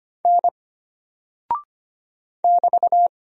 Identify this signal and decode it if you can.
Morse code, heard as a keyed tone, about 25 wpm, 700 Hz. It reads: N <BT>
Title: Single Character (Letter:Number:Punctuation:Prosign) Alt 25wpm